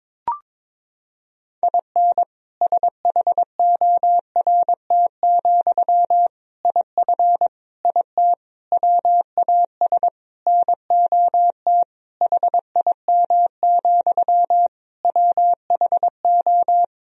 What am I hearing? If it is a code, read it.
Morse code, heard as a keyed tone, about 22 wpm, 700 Hz. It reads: IN SHORT, IF IT WAS NOT HIM, WHO
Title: Sentences from Top 1000 Words - Review Part 2 - 22wpm